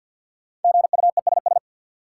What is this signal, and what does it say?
Morse code, heard as a keyed tone, about 53 wpm, 700 Hz. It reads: GUESS
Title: Sentences from Top 1500 Words - New Word Drill Alt - 50wpm